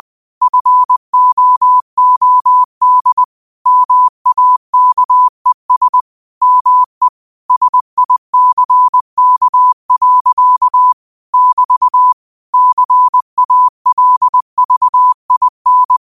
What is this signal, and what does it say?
Morse code, heard as a keyed tone, about 20 wpm, 1000 Hz. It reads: FOOD MAKES ME SICK. <BT> CALVIN
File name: qotdfiles / 20wpm